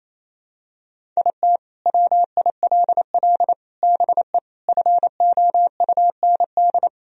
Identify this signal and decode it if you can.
Morse code, heard as a keyed tone, about 28 wpm, 700 Hz. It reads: IT WILL BE FOUND
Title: Sentences from Top 1000 Words - Review Part 1 - 28wpm